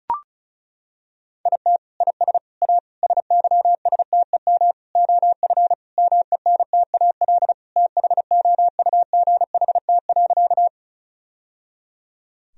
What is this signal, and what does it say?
Morse code, heard as a keyed tone, about 34 wpm, 700 Hz. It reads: IT IS A SYSTEM OF MENTAL THOUGHT.
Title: Sentences from Top 1000 Words - Review Part 4 - 35wpm